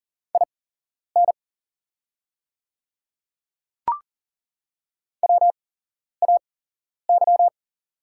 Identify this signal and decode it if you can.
Morse code, heard as a keyed tone, about 40 wpm, spaced as for 11 wpm, 700 Hz. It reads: IN WAY
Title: 8x Character Spacing - Top 100 Words - 40wpm